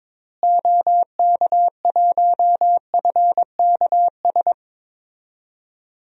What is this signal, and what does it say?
Morse code, heard as a keyed tone, about 22 wpm, 700 Hz. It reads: OK1FKS
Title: DXCC-100 Call Signs - 6 Characters - Repeated 2x - 22wpm